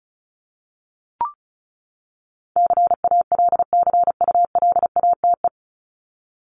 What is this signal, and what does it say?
Morse code, heard as a keyed tone, about 34 wpm, 700 Hz. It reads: CALCULATE
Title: Top 1751-2000 Words 35wpm